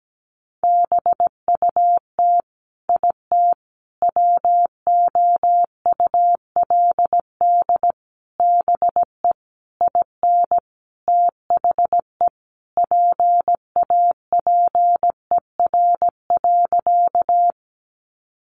Morse code, heard as a keyed tone, about 17 wpm, 700 Hz. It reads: BUT IT WOULD BE IN THE PAPER.
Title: Sentences from Top 300 Words 17wpm